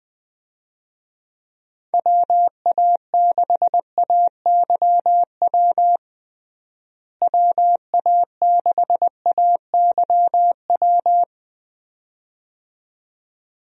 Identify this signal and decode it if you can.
Morse code, heard as a keyed tone, about 20 wpm, 700 Hz. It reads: WA6AYW WA6AYW
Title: USA Call Signs - 6 Characters - Repeated 2x - 20wpm